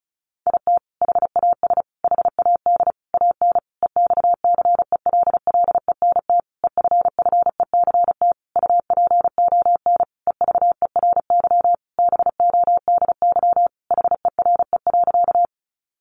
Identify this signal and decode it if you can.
Morse code, heard as a keyed tone, about 34 wpm, 700 Hz. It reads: IT HAS HAD AN EXCELLENT EFFECT UPON EVERY BODY HERE.
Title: Sentences from Top 1100 Words 35wpm